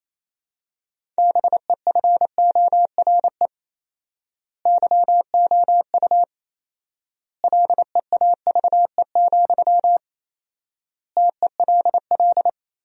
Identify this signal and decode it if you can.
Morse code, heard as a keyed tone, about 28 wpm, 700 Hz. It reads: BEFORE YOU LEAVE, TELL
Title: 4x Word-Spacing - Sentences from Top 400 Words - 28wpm